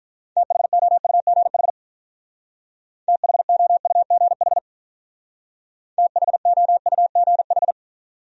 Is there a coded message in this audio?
Yes: Morse code from 0.4 to 7.7 s, heard as a keyed tone, about 46 wpm, 700 Hz. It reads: THOUGH THOUGH THOUGH